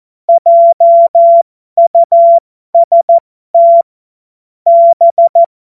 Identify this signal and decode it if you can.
Morse code, heard as a keyed tone, about 14 wpm, spaced as for 12 wpm, 680 Hz. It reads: JUST B
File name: Morsekurs / Witze